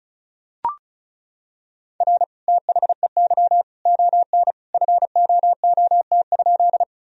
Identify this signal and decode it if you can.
Morse code, heard as a keyed tone, about 34 wpm, 700 Hz. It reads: R THEY ON FOOT?
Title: Sentences from Top 500 Words and Abbreviations 35wpm